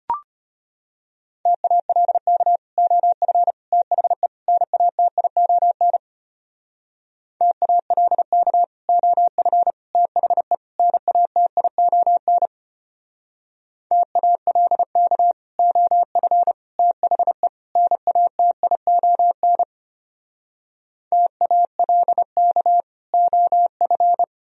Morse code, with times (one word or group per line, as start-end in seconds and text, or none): 1.5-2.6 TALK
2.8-3.5 OF
3.7-4.3 THE
4.5-6.0 NATION
7.4-8.6 TALK
8.9-9.7 OF
9.9-10.5 THE
10.8-12.5 NATION
13.9-15.3 TALK
15.6-16.5 OF
16.8-17.5 THE
17.8-19.6 NATION
21.1-22.8 TALK
23.1-24.2 OF